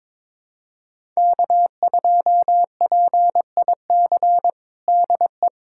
Morse code, heard as a keyed tone, about 22 wpm, 700 Hz. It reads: K2PIC DE
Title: USA Callsign DE Callsign 22wpm